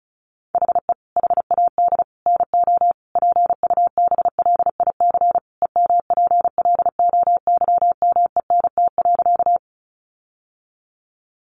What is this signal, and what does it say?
Morse code, heard as a keyed tone, about 34 wpm, 700 Hz. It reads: HE HAD NO PUBLIC EMPLOYMENT.
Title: Sentences from Top 1500 Words 35wpm